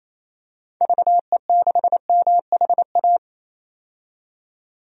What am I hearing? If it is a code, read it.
Morse code, heard as a keyed tone, about 28 wpm, 700 Hz. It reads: VE6MHA
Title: DXCC-100 Call Signs - 6 Characters - 28wpm